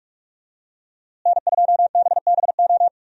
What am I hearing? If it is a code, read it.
Morse code, heard as a keyed tone, about 46 wpm, 700 Hz. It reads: N2BBO
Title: USA Call Signs - 5 Characters - Repeated 3x - 45wpm